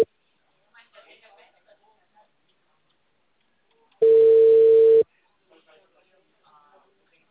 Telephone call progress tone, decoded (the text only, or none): ringback tone